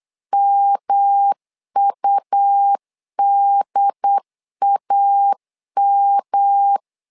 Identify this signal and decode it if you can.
Morse code, heard as a keyed tone, about 8 wpm, 800 Hz. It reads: MUDAM
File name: LEZIONI / 3 MWRO / 3 M 8wpm 2300